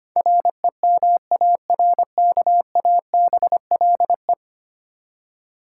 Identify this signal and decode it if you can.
Morse code, heard as a keyed tone, about 25 wpm, 700 Hz. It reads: REMARKABLE
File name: Ten Letter Words - 25wpm